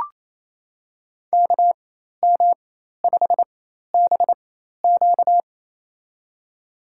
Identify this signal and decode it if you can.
Morse code, heard as a keyed tone, about 28 wpm, spaced as for 13 wpm, 700 Hz. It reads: KM5BQ
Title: USA Call Signs - 5 Characters - 4x Character Spacing - 28wpm